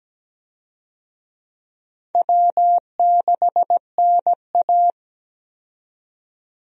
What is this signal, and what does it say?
Morse code, heard as a keyed tone, about 17 wpm, 700 Hz. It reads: W6NA
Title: USA Call Signs - 4 Characters - 17wpm